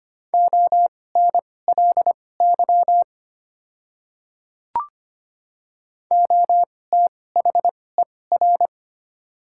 Morse code, heard as a keyed tone, about 25 wpm, spaced as for 18 wpm, 690 Hz. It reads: ONLY OTHER
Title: 2x Character Spacing - Top 200 Words - 25wpm